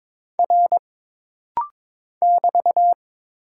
Morse code, heard as a keyed tone, about 21 wpm, 700 Hz. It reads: R <BT>
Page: This rapid-fire timing is geared to push your ICR (Instant Character Recognition) skills.